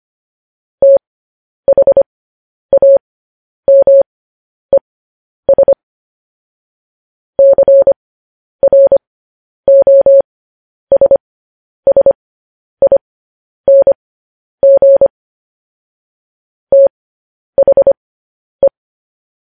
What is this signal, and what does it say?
Morse code, heard as a keyed tone, about 25 wpm, spaced as for 10 wpm, 550 Hz. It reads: THAMES CROSSING THE
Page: news-headlines-in-morse-code-at-10-wpm